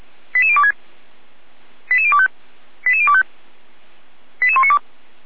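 Five-tone selective call (ZVEI), heard as quick runs of five tones, tones of about 70 ms. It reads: 88026 88025 88025 88192
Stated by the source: ZVEI five-tone call received with an SDR.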